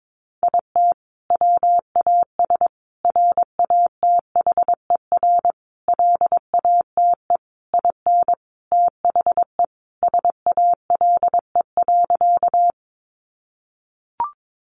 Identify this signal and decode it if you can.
Morse code, heard as a keyed tone, about 22 wpm, 700 Hz. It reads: IT WAS RATHER LATE IN THE SALE.
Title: Sentences from Top 1400 Words 22wpm